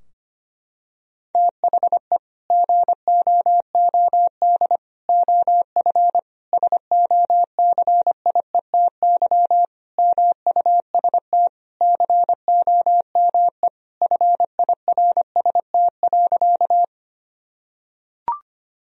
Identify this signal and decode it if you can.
Morse code, heard as a keyed tone, about 25 wpm, 700 Hz. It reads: THE GOOD OF SOCIETY MUST COME FIRST.
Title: Sentences from Top 1000 Words - Review Part 4 - 25wpm